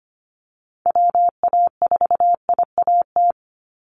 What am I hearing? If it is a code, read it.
Morse code, heard as a keyed tone, about 25 wpm, 700 Hz. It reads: WA4IAT
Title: USA Call Signs 25wpm